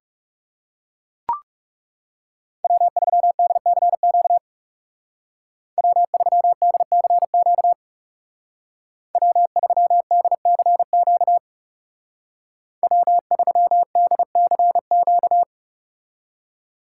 Morse code, heard as a keyed tone, about 39 wpm, 700 Hz. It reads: W3DCQ W3DCQ W3DCQ W3DCQ